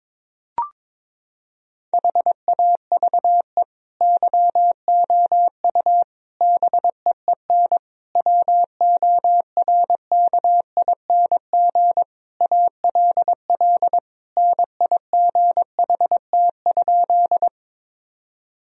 Morse code, heard as a keyed tone, about 22 wpm, 700 Hz. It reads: HAVE YOU BEEN WORKING ALL NIGHT?
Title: Sentences from Top 1000 Words - Review Part 4 - 22wpm